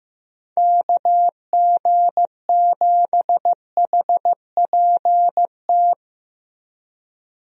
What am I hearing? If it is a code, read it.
Morse code, heard as a keyed tone, about 15 wpm, 700 Hz. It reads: KG7HPT
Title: USA Call Signs - 6 Characters - 15wpm